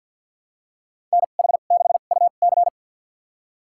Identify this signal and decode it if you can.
Morse code, heard as a keyed tone, about 53 wpm, spaced as for 37 wpm, 700 Hz. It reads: NH6UC
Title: USA Call Signs - 5 Characters - 2x Character Spacing - 50wpm